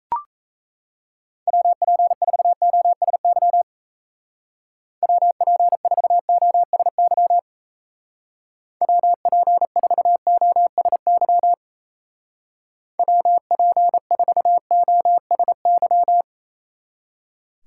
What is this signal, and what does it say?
Morse code, heard as a keyed tone, about 36 wpm, 700 Hz. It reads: WP4OSY WP4OSY WP4OSY WP4OSY